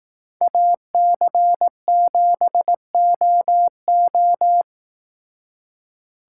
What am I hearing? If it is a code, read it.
Morse code, heard as a keyed tone, about 18 wpm, 700 Hz. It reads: AC7OO
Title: Speed Racing - USA 2x2 Call Signs 15wpm